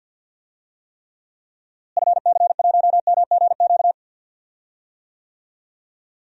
Morse code, heard as a keyed tone, about 48 wpm, 700 Hz. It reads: FC1MGQ
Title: DXCC-100 Call Signs - 6 Characters - Repeated 2x - 50wpm